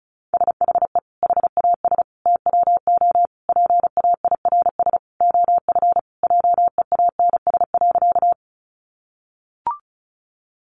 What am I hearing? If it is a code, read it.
Morse code, heard as a keyed tone, about 34 wpm, 700 Hz. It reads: SHE HAS TWO PAIRS OF JEANS.
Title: Additional Easy Phrases 35wpm